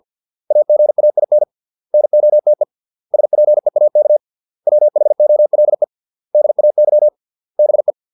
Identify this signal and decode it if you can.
Morse code, heard as a keyed tone, about 50 wpm, 600 Hz. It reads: AGAIN NOTE SPEAK WHOLE DAY BE